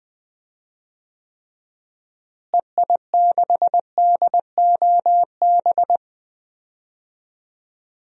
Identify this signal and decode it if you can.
Morse code, heard as a keyed tone, about 20 wpm, 700 Hz. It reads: EI6DOB
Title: DXCC-100 Call Signs - 6 Characters - Repeated 3x - 20wpm